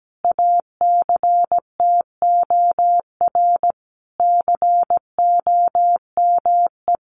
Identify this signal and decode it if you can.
Morse code, heard as a keyed tone, about 17 wpm, 700 Hz. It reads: ACTOR COME
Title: Sentences from Top 1500 Words 17wpm